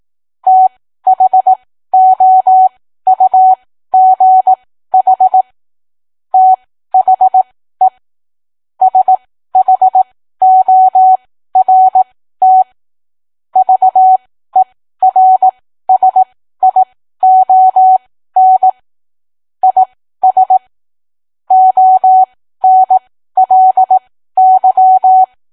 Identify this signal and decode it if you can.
Morse code, heard as a keyed tone, about 18 wpm, spaced as for 13 wpm, 750 Hz. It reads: THOUGH THE SHORT VERSION IS ONLY